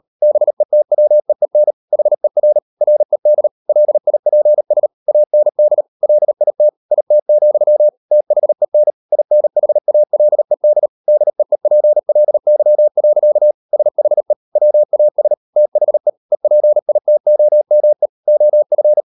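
Morse code, heard as a keyed tone, about 38 wpm, 600 Hz. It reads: BETWEEN HER RED LIPS AND LIT IT, THEN INHALED DEEPLY. SHE WAS THE EPITOME OF
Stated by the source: Morse code practice at 38 WPM, sending Ken Follett's novel Never in English, from CW med Gnister.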